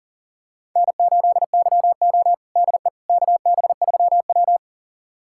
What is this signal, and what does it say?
Morse code, heard as a keyed tone, about 40 wpm, 700 Hz. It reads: N8YO DE KB3W
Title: USA Callsign DE Callsign - 4 Characters - 40wpm